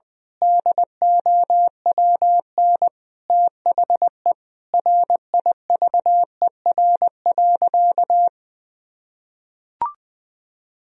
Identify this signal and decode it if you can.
Morse code, heard as a keyed tone, about 20 wpm, 700 Hz. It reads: DOWN THE RIVER.
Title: Sentences from Top 1000 Words - Review Part 1 - 20wpm